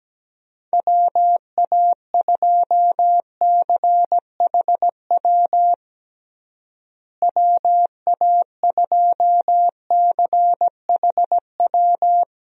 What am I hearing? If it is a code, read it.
Morse code, heard as a keyed tone, about 17 wpm, 700 Hz. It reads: WA2CHW WA2CHW
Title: USA Call Signs - 6 Characters - Repeated 3x - 17wpm